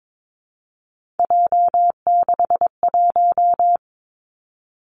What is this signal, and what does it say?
Morse code, heard as a keyed tone, about 22 wpm, 700 Hz. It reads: J61
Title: Three Challenging Letter-Number Combinations 22wpm